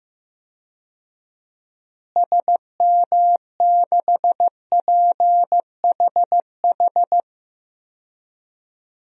Morse code, heard as a keyed tone, about 15 wpm, 700 Hz. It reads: SM6PHH
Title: DXCC-100 Call Signs - 6 Characters - Repeated 2x - 15wpm